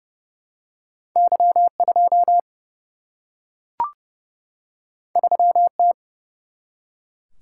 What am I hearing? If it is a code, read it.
Morse code, heard as a keyed tone, about 30 wpm, 700 Hz. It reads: Y2 3T
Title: Single-Letter Single-Number 30wpm